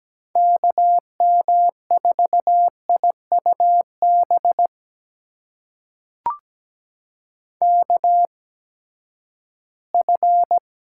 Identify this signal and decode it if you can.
Morse code, heard as a keyed tone, about 17 wpm, spaced as for 12 wpm, 700 Hz. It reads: KM4IUB KF
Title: USA Call Signs - 6 Characters - 8x Character Spacing - 17wpm